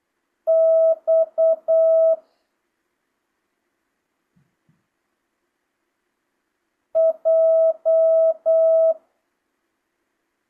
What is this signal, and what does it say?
Morse code, heard as a keyed tone, about 8 wpm, 650 Hz. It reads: X J